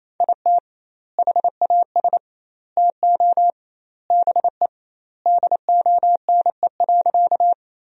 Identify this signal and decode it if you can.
Morse code, heard as a keyed tone, about 28 wpm, 700 Hz. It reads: IT HAS TO BE DONE.
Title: 2x Word-Spacing - Sentences from Top 500 Words - 28wpm